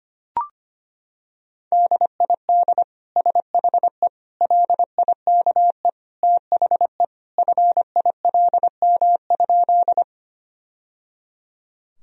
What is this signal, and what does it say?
Morse code, heard as a keyed tone, about 25 wpm, 700 Hz. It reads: DID SHE LIKE THE FILM?
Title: Sentences from Top 1000 Words - Review Part 3 - 25wpm